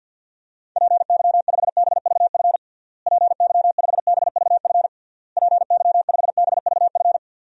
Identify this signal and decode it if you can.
Morse code, heard as a keyed tone, about 49 wpm, 700 Hz. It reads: PY5BVF PY5BVF PY5BVF
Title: DXCC-100 Call Signs - 6 Characters - Repeated 3x - 50wpm